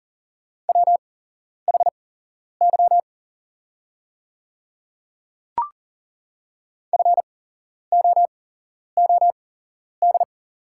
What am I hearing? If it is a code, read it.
Morse code, heard as a keyed tone, about 40 wpm, spaced as for 11 wpm, 700 Hz. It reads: WHY FOOD
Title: 8x Character Spacing - Top 300 Words - 40wpm